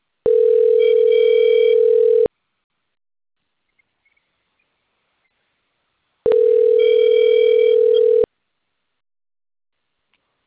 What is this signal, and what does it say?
Ringback tone (North America), 440 + 480 Hz.